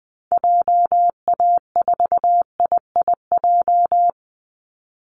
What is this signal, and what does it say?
Morse code, heard as a keyed tone, about 20 wpm, 700 Hz. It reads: JA4IIJ
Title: Call Signs Repeated 3x - 20wpm